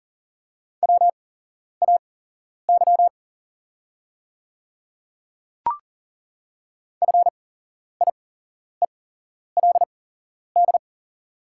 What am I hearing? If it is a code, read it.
Morse code, heard as a keyed tone, about 40 wpm, spaced as for 11 wpm, 690 Hz. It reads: WAY FIELD
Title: 8x Character Spacing - Top 500 Words - 40wpm